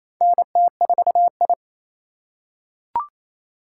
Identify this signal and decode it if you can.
Morse code, heard as a keyed tone, about 28 wpm, 700 Hz. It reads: NT4I
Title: USA Call Signs - 4 Characters - 4x Character Spacing - 28wpm